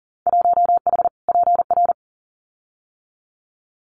Morse code, heard as a keyed tone, about 38 wpm, 700 Hz. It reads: 1H PR
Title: Winter Field Day Exchanges Encore 40wpm